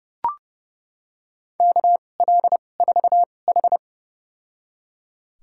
Morse code, heard as a keyed tone, about 31 wpm, spaced as for 22 wpm, 700 Hz. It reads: KL4H
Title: USA Call Signs - 4 Characters - 2x Character Spacing - 30wpm